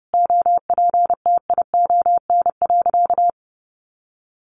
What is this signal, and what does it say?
Morse code, heard as a keyed tone, about 30 wpm, 700 Hz. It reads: OPTION.
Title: Sentences from Top 1300 Words 30wpm